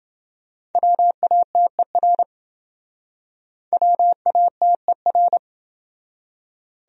Morse code, heard as a keyed tone, about 27 wpm, 700 Hz. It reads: WATER WATER